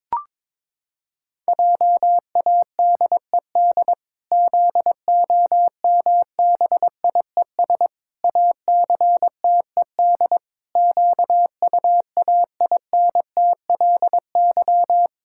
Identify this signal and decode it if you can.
Morse code, heard as a keyed tone, about 22 wpm, 700 Hz. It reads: JADED ZOMBIES ACTED QUAINTLY
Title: Pangram Challenge v2 - 22wpm